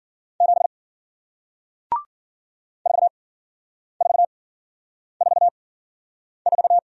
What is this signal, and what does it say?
Morse code, heard as a keyed tone, about 54 wpm, spaced as for 9 wpm, 700 Hz. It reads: 7 4444